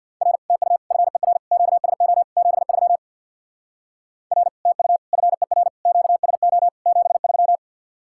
Morse code, heard as a keyed tone, about 60 wpm, 700 Hz. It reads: R TU FER QSO 73 R TU FER QSO 73
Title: Speed-Racing Common QSO Phrases - 50wpm